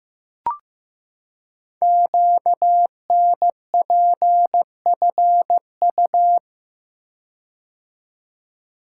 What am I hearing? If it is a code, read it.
Morse code, heard as a keyed tone, about 15 wpm, 700 Hz. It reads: QNPFU
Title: Code Groups - 5 Letters - 15wpm